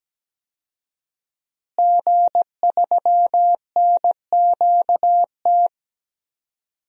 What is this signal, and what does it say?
Morse code, heard as a keyed tone, about 17 wpm, 700 Hz. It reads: G3NQT
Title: DXCC-100 Call Signs - 5 Characters - Repeated 2x - 17wpm